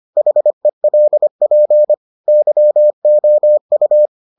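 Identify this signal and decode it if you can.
Morse code, heard as a keyed tone, about 25 wpm, 600 Hz. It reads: HELP YOU
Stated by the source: Morse edition of the Foundations of Amateur Radio episodes.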